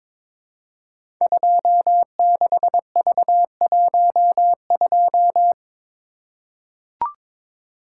Morse code, heard as a keyed tone, about 22 wpm, 700 Hz. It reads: 26V12
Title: Code Groups - 5 Letter:Numbers - 22wpm